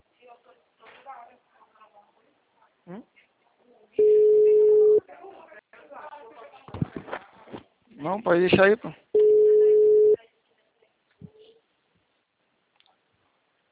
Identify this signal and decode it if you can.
Ringback tone (Europe, 425 Hz), with speech.
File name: out- -2013- - - .64611